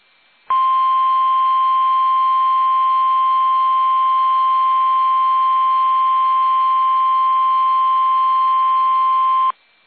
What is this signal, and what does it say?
Emergency Alert System (SAME) tones (the 1050 Hz attention tone). It is heard from 0.4 to 9.6 s.
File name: eas / KUSO / KUSO